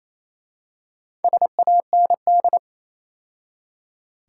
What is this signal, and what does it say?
Morse code, heard as a keyed tone, about 27 wpm, 700 Hz. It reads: SAND
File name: Easy Words - Speed-Racing with Hints - 22wpm